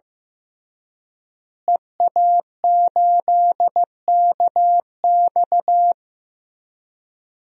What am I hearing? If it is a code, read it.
Morse code, heard as a keyed tone, about 15 wpm, 700 Hz. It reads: EA8KX
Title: DXCC-100 Call Signs - 5 Characters - Repeated 3x - 15wpm